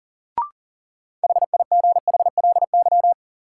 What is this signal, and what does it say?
Morse code, heard as a keyed tone, about 40 wpm, 700 Hz. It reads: HIGHLY
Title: Sentences from Top 1100 Words - New Word Drill Alt - 40wpm